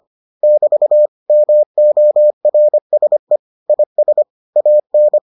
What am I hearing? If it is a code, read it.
Morse code, heard as a keyed tone, about 25 wpm, 600 Hz. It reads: <BT> MORSE IS AN